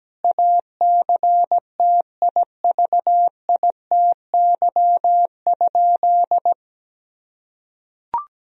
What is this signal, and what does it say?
Morse code, heard as a keyed tone, about 17 wpm, 700 Hz. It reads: ACTIVITY?
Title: Sentences from Top 1000 Words - Review Part 4 - 17wpm